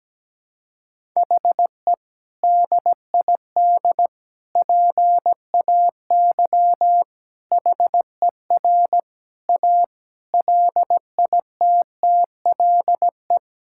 Morse code, heard as a keyed tone, about 17 wpm, 700 Hz. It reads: HE DID PAY HER A LITTLE
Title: Sentences from Top 1000 Words - Review Part 6 - 17wpm